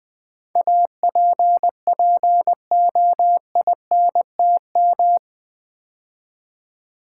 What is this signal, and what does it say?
Morse code, heard as a keyed tone, about 20 wpm, 700 Hz. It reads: APPOINTM
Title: Build A Word - 4 Letters At A Time - 11-Letter Words - 20wpm